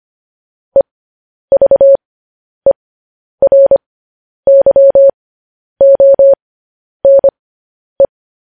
Morse code, heard as a keyed tone, about 25 wpm, spaced as for 10 wpm, 550 Hz. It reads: EVERYONE
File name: podcast / News headlines-1x-25x10